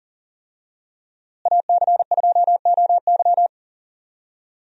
Morse code, heard as a keyed tone, about 40 wpm, 700 Hz. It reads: AC2OY
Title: USA Call Signs - 5 Characters - Repeated 2x - 40wpm